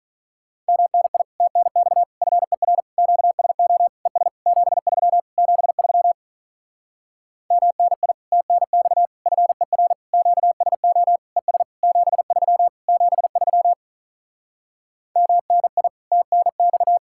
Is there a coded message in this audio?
Yes: Morse code from 0.7 to 17.0 s, heard as a keyed tone, about 47 wpm, 700 Hz. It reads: MNI TNX FER QSO ES 73 73 MNI TNX FER QSO ES 73 73 MNI TNX